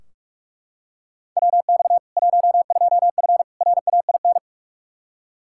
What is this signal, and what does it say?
Morse code, heard as a keyed tone, about 43 wpm, 700 Hz. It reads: WX 12F RAIN
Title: K1USN SST QSO Practice 45wpm